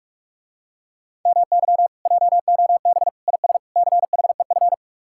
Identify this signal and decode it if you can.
Morse code, heard as a keyed tone, about 45 wpm, 700 Hz. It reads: MY JOB IS CHEF